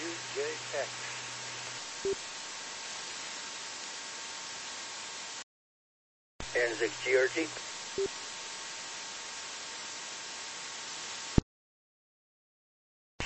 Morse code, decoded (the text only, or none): none